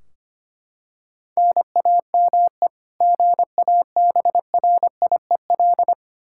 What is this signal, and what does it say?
Morse code, heard as a keyed tone, about 25 wpm, 700 Hz. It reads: NAME GABRIEL
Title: K1USN SST QSO Practice 25wpm